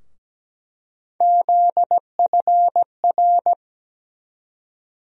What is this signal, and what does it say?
Morse code, heard as a keyed tone, about 17 wpm, 700 Hz. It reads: ZFR
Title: USA Call Sign Postfixes - 3 Letters 17wpm